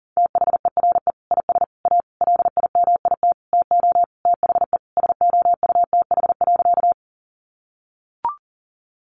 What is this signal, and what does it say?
Morse code, heard as a keyed tone, about 40 wpm, 700 Hz. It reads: THERE IS A LIMIT TO THE SOUTH.
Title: Sentences from Top 1500 Words 40wpm